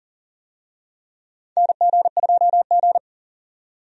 Morse code, heard as a keyed tone, about 40 wpm, 700 Hz. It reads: NG2G
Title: USA Call Signs - 4 Characters - 40wpm